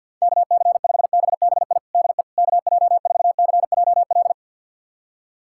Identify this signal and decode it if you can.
Morse code, heard as a keyed tone, about 49 wpm, 690 Hz. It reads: KC5BBI DE KJ4CJL
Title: USA Callsign DE Callsign - 6 Characters - 50wpm